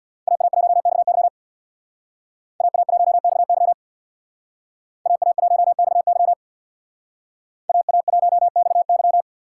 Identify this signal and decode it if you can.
Morse code, heard as a keyed tone, about 61 wpm, 700 Hz. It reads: AA1XY AA1XY AA1XY AA1XY
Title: Speed Racing - USA 2x2 Call Signs 50wpm